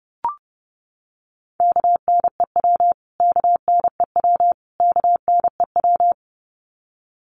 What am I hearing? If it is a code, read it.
Morse code, heard as a keyed tone, about 30 wpm, 700 Hz. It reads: KNEW KNEW KNEW
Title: Top 500 Words Repeated 30wpm